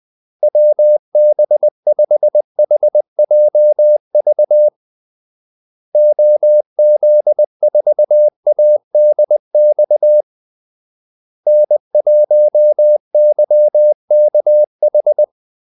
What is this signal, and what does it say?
Morse code, heard as a keyed tone, about 20 wpm, 600 Hz. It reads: WB5HJV OZ4ADX N1YKH
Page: Kaldesignaler 20 wpm